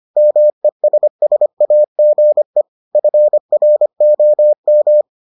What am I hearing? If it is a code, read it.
Morse code, heard as a keyed tone, about 25 wpm, 600 Hz. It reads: MESSAGE FROM